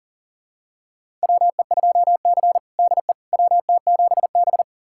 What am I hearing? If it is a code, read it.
Morse code, heard as a keyed tone, about 38 wpm, 700 Hz. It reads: WE2C DE WT7B
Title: USA Callsign DE Callsign - 4 Characters - 40wpm